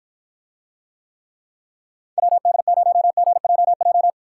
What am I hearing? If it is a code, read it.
Morse code, heard as a keyed tone, about 53 wpm, 700 Hz. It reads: WD0GJJ